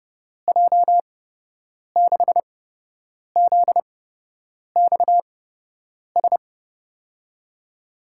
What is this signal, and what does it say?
Morse code, heard as a keyed tone, about 30 wpm, spaced as for 8 wpm, 700 Hz. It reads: J6ZXS
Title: USA Call Signs - 6 Characters - 8x Character Spacing - 30wpm